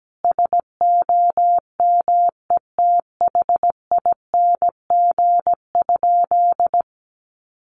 Morse code, heard as a keyed tone, about 17 wpm, 700 Hz. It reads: SOMETHING?
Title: Sentences from Top 1100 Words 17wpm